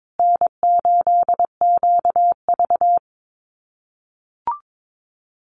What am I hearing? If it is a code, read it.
Morse code, heard as a keyed tone, about 22 wpm, 700 Hz. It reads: N8QV
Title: USA Call Signs Repeated 2x 22wpm